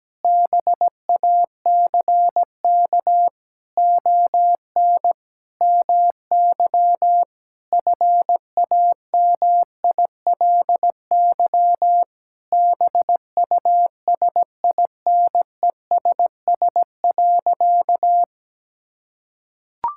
Morse code, heard as a keyed tone, about 17 wpm, 700 Hz. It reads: BACK ON MY FAMILY BUSINESS.
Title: Sentences from Top 1000 Words - Review Part 6 - 17wpm